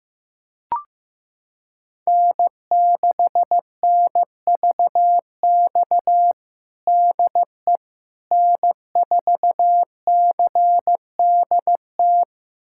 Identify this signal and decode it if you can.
Morse code, heard as a keyed tone, about 15 wpm, 700 Hz. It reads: N6NVX DE N4CDT
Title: USA Callsign DE Callsign - 5 Characters - 15wpm